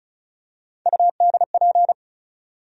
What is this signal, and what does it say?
Morse code, heard as a keyed tone, about 36 wpm, 700 Hz. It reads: UDP
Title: USA Call Sign Postfixes - 3 Letters 35wpm